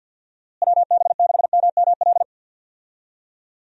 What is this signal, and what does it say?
Morse code, heard as a keyed tone, about 49 wpm, 700 Hz. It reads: WB6MML
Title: USA Call Signs - 6 Characters - Repeated 3x - 50wpm